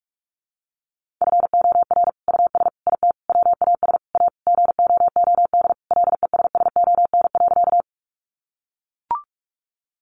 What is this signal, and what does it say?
Morse code, heard as a keyed tone, about 44 wpm, 700 Hz. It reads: FOR US IT WAS A GOOD LESSON.
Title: Sentences from Top 1200 Words 45wpm